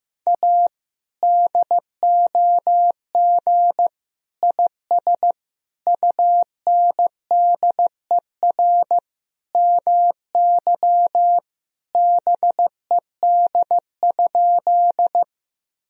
Morse code, heard as a keyed tone, about 16 wpm, 700 Hz. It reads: A DOG IS UNDER MY BED?
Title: Sentences from Top 1000 Words - Review Part 2 - 15wpm